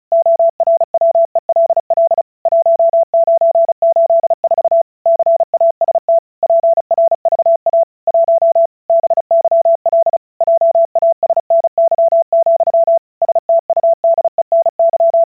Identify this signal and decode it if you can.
Morse code, heard as a keyed tone, about 36 wpm, 670 Hz. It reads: ORWELL 1984 CAST PRVA 1 BYL JASNY, STUDENY
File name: vyuka cw / 35wpm / ukazka 35wpm